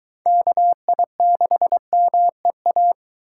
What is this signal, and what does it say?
Morse code, heard as a keyed tone, about 22 wpm, 700 Hz. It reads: KI6MEA